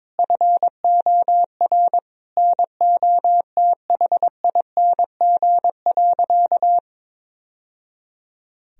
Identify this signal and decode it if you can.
Morse code, heard as a keyed tone, about 22 wpm, 700 Hz. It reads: FOR NOTHING.
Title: Sentences from Top 1000 Words - Review Part 3 - 22wpm